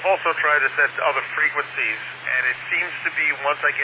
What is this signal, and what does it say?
Emergency Alert System (SAME) tones, heard as harsh buzzing bursts, with speech. It is heard from 0.4 to 3.4 s.